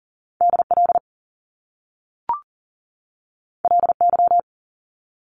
Morse code, heard as a keyed tone, about 40 wpm, 700 Hz. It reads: DL LY
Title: Top Two Letter Combinations in Words 40wpm